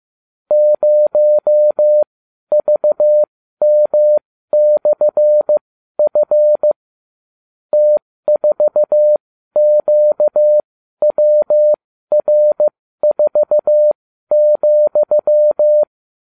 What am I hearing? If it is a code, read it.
Morse code, heard as a keyed tone, about 15 wpm, spaced as for 12 wpm, 600 Hz. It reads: 0VM/F T4QWR4,